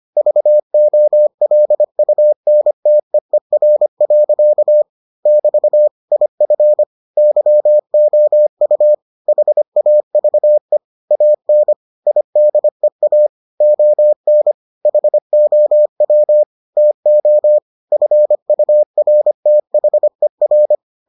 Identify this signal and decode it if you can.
Morse code, heard as a keyed tone, about 25 wpm, 600 Hz. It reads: VOLUNTEER. <BT> IF YOU HAVE AN IDEA ON HOW TO FURTHER